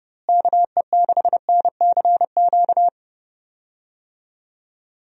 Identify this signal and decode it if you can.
Morse code, heard as a keyed tone, about 32 wpm, 700 Hz. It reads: KE6NCQ
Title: USA Call Signs - 6 Characters - Repeated 2x - 30wpm